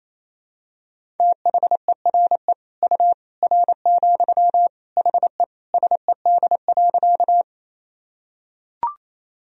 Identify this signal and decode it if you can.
Morse code, heard as a keyed tone, about 28 wpm, 700 Hz. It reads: THERE U R, HE SED.
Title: Sentences from Top 100 Words and Abbreviations 28wpm